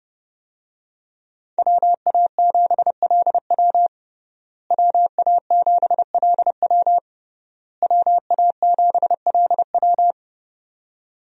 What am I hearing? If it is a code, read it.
Morse code, heard as a keyed tone, about 30 wpm, 700 Hz. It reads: WA7LW WA7LW WA7LW
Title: USA Call Signs - 5 Characters - Repeated 3x - 30wpm